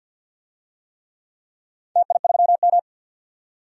Morse code, heard as a keyed tone, about 51 wpm, 700 Hz. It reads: TI3M
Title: DXCC-100 Call Signs - 4 Characters - 50wpm